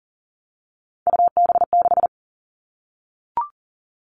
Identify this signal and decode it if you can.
Morse code, heard as a keyed tone, about 40 wpm, 700 Hz. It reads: UB6
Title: All Cut Numbers - 3-Digit 40wpm